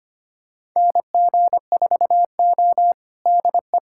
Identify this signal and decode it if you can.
Morse code, heard as a keyed tone, about 25 wpm, 700 Hz. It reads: NG4O DE
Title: USA Callsign DE Callsign - 4 Characters - 25wpm